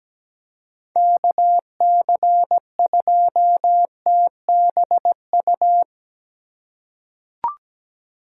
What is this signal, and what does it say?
Morse code, heard as a keyed tone, about 17 wpm, 700 Hz. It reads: KC2TBU
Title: USA Call Signs - 2x Character Spacing - 17wpm